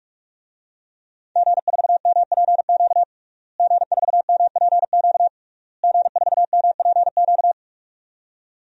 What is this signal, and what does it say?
Morse code, heard as a keyed tone, about 44 wpm, 700 Hz. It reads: G4MPQ G4MPQ G4MPQ
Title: DXCC-100 Call Signs - 5 Characters - Repeated 3x - 45wpm